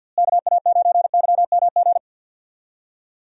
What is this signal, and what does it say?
Morse code, heard as a keyed tone, about 46 wpm, 700 Hz. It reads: KA9YMG
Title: USA Call Signs - 6 Characters - Repeated 3x - 50wpm